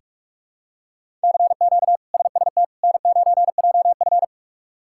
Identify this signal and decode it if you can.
Morse code, heard as a keyed tone, about 42 wpm, 700 Hz. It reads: CQ SST N9JF
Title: K1USN SST QSO Practice - 45wpm